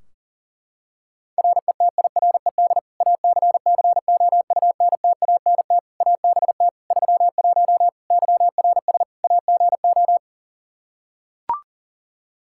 Morse code, heard as a keyed tone, about 37 wpm, 700 Hz. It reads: RETIRED ACCOUNTANT ABT 31 YRS AGO
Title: K1USN SST QSO Practice 40wpm